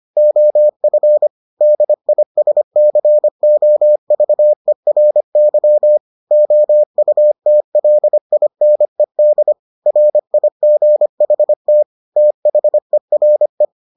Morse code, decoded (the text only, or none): OF DISCOVERY OUTLINED RIGHT THERE